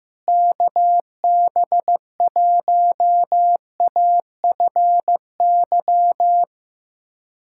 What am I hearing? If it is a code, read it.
Morse code, heard as a keyed tone, about 15 wpm, 700 Hz. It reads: KB1AFY
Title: USA Callsign DE Callsign - 6 Characters - 15wpm